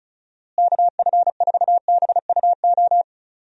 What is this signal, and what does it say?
Morse code, heard as a keyed tone, about 36 wpm, 700 Hz. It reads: KF4BUO
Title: USA Call Signs - 6 Characters - Repeated 3x - 35wpm